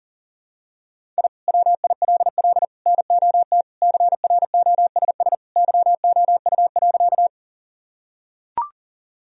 Morse code, heard as a keyed tone, about 40 wpm, 700 Hz. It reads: I WILL NOT CROSS YOU.
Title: Sentences from Top 1000 Words - Review Part 1 - 40wpm